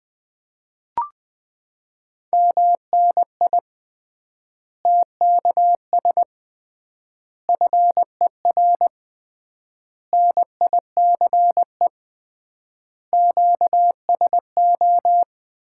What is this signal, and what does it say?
Morse code, heard as a keyed tone, about 21 wpm, 700 Hz. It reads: MNI TKS FER NICE QSO
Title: Common QSO Phrases - 3x Word-Spacing - 20wpm